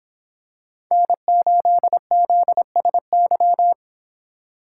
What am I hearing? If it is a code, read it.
Morse code, heard as a keyed tone, about 26 wpm, 700 Hz. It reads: N8ZSY